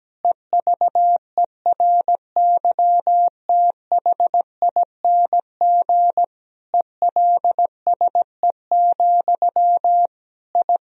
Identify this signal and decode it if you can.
Morse code, heard as a keyed tone, about 17 wpm, 700 Hz. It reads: EVERYTHING ELSE, I
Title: Sentences from Top 1000 Words - Review Part 2 - 17wpm